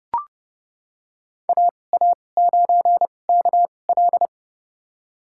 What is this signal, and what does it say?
Morse code, heard as a keyed tone, about 31 wpm, spaced as for 22 wpm, 700 Hz. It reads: AA9KL
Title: USA Call Signs - 5 Characters - 2x Character Spacing - 30wpm